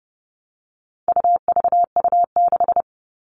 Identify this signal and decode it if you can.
Morse code, heard as a keyed tone, about 30 wpm, 700 Hz. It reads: UVU6
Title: All Cut Numbers - 4-Digit 30wpm